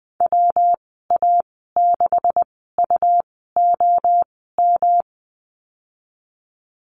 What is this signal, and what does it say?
Morse code, heard as a keyed tone, about 19 wpm, spaced as for 14 wpm, 700 Hz. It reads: WA6UOM
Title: 167 Lesson 27g - Number 7 - Call Signs 20wpm